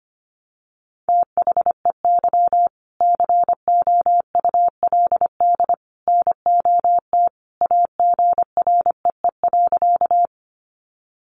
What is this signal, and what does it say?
Morse code, heard as a keyed tone, about 25 wpm, 700 Hz. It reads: THEY COULD NOT AGREE.
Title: Sentences from Top 1100 Words 25wpm